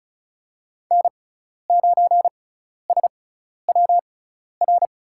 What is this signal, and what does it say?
Morse code, heard as a keyed tone, about 34 wpm, spaced as for 12 wpm, 700 Hz. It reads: N9SWR
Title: USA Call Signs - 5 Characters - 6x Character Spacing - 35wpm